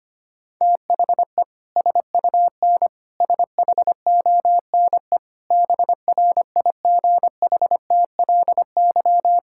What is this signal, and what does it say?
Morse code, heard as a keyed tone, about 25 wpm, 700 Hz. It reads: THE SUN SHONE BRIGHTLY
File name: practice / 25wpm / Sentences with 2-Word Phrases - 25wpm